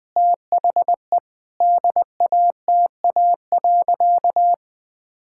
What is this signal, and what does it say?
Morse code, heard as a keyed tone, about 20 wpm, 700 Hz. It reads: THE DATA.
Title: Sentences with 7-Word Phrases - Spoken Beginning - 20wpm